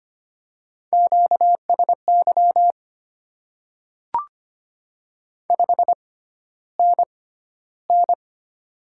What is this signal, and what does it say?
Morse code, heard as a keyed tone, about 25 wpm, spaced as for 13 wpm, 690 Hz. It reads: QSY 5NN
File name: Top 90 QSO Elements - 6x Character Spacing - 25wpm